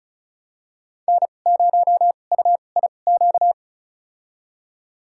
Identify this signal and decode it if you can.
Morse code, heard as a keyed tone, about 34 wpm, spaced as for 25 wpm, 700 Hz. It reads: N0UIQ
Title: USA Call Signs - 5 Characters - 2x Character Spacing - 35wpm